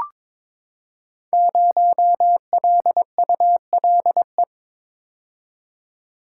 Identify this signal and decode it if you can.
Morse code, heard as a keyed tone, about 22 wpm, 700 Hz. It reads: 0LULE
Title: Code Groups - 5 Letter:Numbers - 22wpm